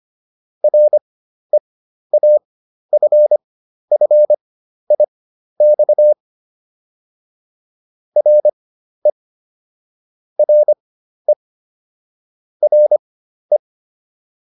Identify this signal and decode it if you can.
Morse code, heard as a keyed tone, about 25 wpm, spaced as for 12 wpm, 600 Hz. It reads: REAFFIX RE RE RE